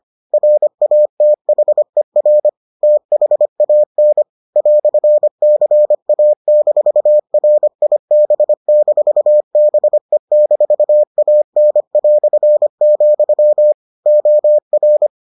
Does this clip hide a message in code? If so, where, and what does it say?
Yes: Morse code from 0.3 to 15.1 s, heard as a keyed tone, about 25 wpm, 600 Hz. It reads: RATHER THAN "CA-RIB-BE-AN", OR